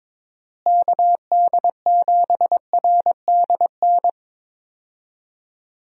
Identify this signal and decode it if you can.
Morse code, heard as a keyed tone, about 22 wpm, 700 Hz. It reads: KD7RDN
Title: USA Call Signs - 6 Characters - Repeated 3x - 22wpm